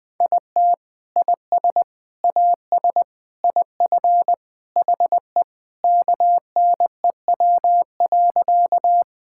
Morse code, heard as a keyed tone, about 21 wpm, 700 Hz. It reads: IT IS AS IF HE KNEW.
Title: Sentences with 3-Word Phrases - 20wpm